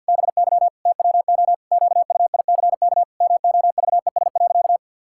Morse code, heard as a keyed tone, about 52 wpm, 700 Hz. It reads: BY TWO QUICK MOVES.
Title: Pangram Challenge v2 - 50wpm